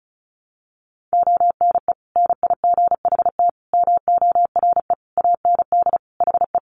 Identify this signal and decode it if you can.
Morse code, heard as a keyed tone, about 34 wpm, 700 Hz. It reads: ONE NIGHT MORE AND HE
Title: Sentences from Top 300 Words 35wpm